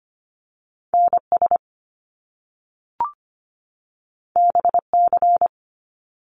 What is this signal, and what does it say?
Morse code, heard as a keyed tone, about 25 wpm, 700 Hz. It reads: NS BC
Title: Canadian Provinces and Territories 25wpm